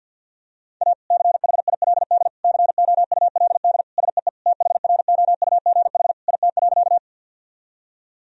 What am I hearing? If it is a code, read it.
Morse code, heard as a keyed tone, about 51 wpm, 700 Hz. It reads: A CHILD COULD SEE THROUGH IT.
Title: Sentences from Top 1000 Words - Review Part 1 - 50wpm